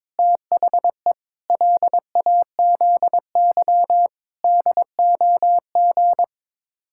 Morse code, heard as a keyed tone, about 22 wpm, 700 Hz. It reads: THE LAZY DOG
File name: practice / 22wpm / Sets of 8 Words - 22wpm